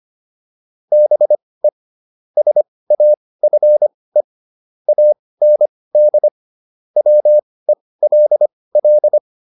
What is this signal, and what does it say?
Morse code, heard as a keyed tone, about 25 wpm, spaced as for 18 wpm, 600 Hz. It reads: BE SAFE AND WELL
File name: PR305 18